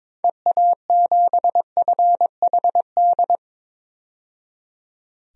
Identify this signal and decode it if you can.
Morse code, heard as a keyed tone, about 22 wpm, 700 Hz. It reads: EA7FHD
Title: DXCC-100 Call Signs - 6 Characters - 22wpm